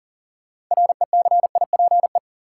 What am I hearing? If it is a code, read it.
Morse code, heard as a keyed tone, about 40 wpm, 700 Hz. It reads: RECIPE
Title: Sentences from Top 1200 Words - New Word Drill Alt - 40wpm